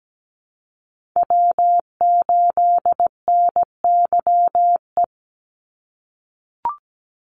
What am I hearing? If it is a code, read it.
Morse code, heard as a keyed tone, about 17 wpm, 700 Hz. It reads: W8NYE
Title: USA Call Signs 17wpm